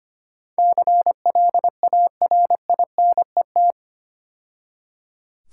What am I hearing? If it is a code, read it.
Morse code, heard as a keyed tone, about 28 wpm, 700 Hz. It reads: CLARINET
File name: Easy Words - Speed-Racing with Hints - 25wpm